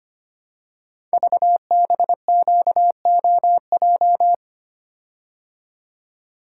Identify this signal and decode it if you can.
Morse code, heard as a keyed tone, about 25 wpm, 700 Hz. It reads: VBQOJ
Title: Code Groups - 5 Letters - 25wpm